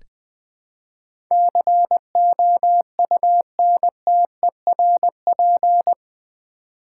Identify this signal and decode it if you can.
Morse code, heard as a keyed tone, about 20 wpm, 700 Hz. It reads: COUNTERP
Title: Build A Word - 4 Letters At A Time - 11-Letter Words - 20wpm